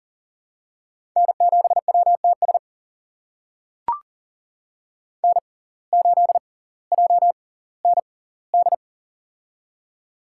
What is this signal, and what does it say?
Morse code, heard as a keyed tone, about 40 wpm, spaced as for 20 wpm, 700 Hz. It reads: N7WTS N8JND
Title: USA Call Signs - 5 Characters - 6x Character Spacing - 40wpm